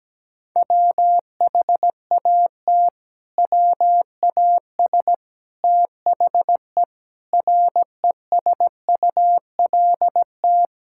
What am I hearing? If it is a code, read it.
Morse code, heard as a keyed tone, about 17 wpm, 700 Hz. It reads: WHAT WAS THE RESULT
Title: Sentences from Top 1000 Words - Review Part 5 - 17wpm